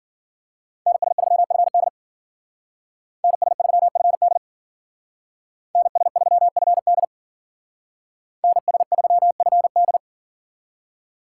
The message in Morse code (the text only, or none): NS3FD NS3FD NS3FD NS3FD